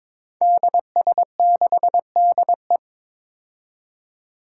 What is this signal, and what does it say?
Morse code, heard as a keyed tone, about 22 wpm, 700 Hz. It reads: DS6DE
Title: Five Challenging Letter-Number Combinations 22wpm